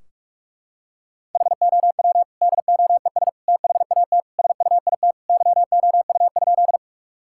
Morse code, heard as a keyed tone, about 46 wpm, 700 Hz. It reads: HOW DOES THAT SUIT YOU?
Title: Sentences from Top 1600 Words 45wpm